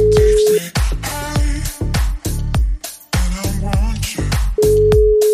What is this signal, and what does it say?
Ringback tone (Europe, 425 Hz).